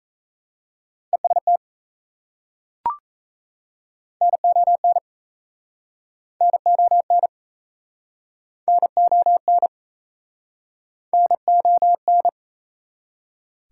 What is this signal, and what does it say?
Morse code, heard as a keyed tone, about 36 wpm, 700 Hz. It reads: EST NON NON NON NON